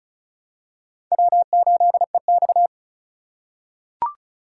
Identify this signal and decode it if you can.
Morse code, heard as a keyed tone, about 36 wpm, 700 Hz. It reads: W8EX
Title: USA Call Signs - 4 Characters - 4x Character Spacing - 35wpm